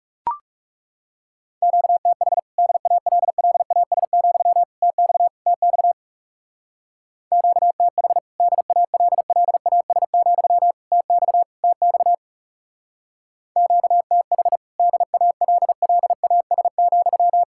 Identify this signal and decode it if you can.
Morse code, heard as a keyed tone, about 40 wpm, 700 Hz. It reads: QTH DALLAS, TX TX QTH DALLAS, TX TX QTH DALLAS,